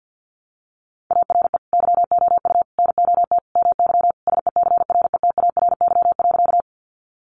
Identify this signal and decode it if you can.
Morse code, heard as a keyed tone, about 48 wpm, 700 Hz. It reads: ARE YOU NOT MY SECRETARY.
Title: Sentences from Top 1300 Words 50wpm